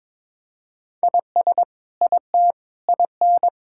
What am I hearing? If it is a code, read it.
Morse code, heard as a keyed tone, about 22 wpm, 700 Hz. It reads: IS IT IN
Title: Sentences from Top 1600 Words 22wpm